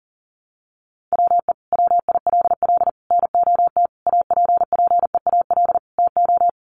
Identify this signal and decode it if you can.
Morse code, heard as a keyed tone, about 40 wpm, 700 Hz. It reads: WE WILL NOT APPEAL TO
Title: Sentences from Top 1300 Words 40wpm